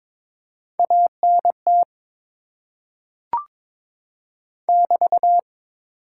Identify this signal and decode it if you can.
Morse code, heard as a keyed tone, about 22 wpm, 700 Hz. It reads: ANT <BT>
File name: Top 90 QSO Elements - 8x Character Spacing - 22wpm